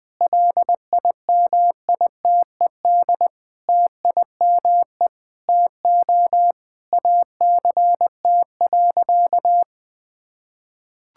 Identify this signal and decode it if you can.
Morse code, heard as a keyed tone, about 20 wpm, 700 Hz. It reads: LIMITED TIME TO ACT.
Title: Sentences from Top 1000 Words - Review Part 6 - 20wpm